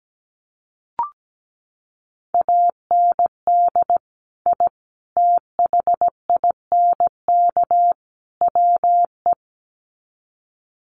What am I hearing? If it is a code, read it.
Morse code, heard as a keyed tone, about 17 wpm, 700 Hz. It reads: AND I THINK WE
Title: Sets of 4 Words 17wpm